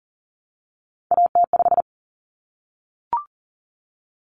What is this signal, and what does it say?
Morse code, heard as a keyed tone, about 40 wpm, 700 Hz. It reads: AT5
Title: Common Cut Numbers - 3-Digit 40wpm